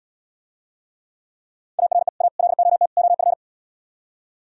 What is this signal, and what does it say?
Morse code, heard as a keyed tone, about 77 wpm, 700 Hz. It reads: ARE A LOT OF